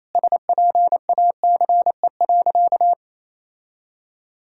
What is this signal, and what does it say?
Morse code, heard as a keyed tone, about 28 wpm, 700 Hz. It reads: SPACE.
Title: Sentences from Top 1000 Words - Review Part 5 - 28wpm